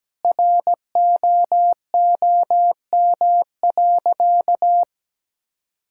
Morse code, heard as a keyed tone, about 17 wpm, 700 Hz. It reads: ROOM.
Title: Sentences from Top 1000 Words - Review Part 6 - 17wpm